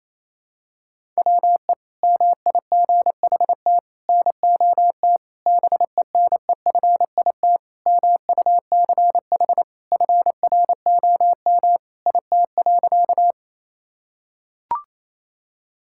Morse code, heard as a keyed tone, about 28 wpm, 700 Hz. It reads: WE MIGHT NOT BENEFIT MUCH FROM IT.
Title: Sentences from Top 1000 Words - Review Part 6 - 28wpm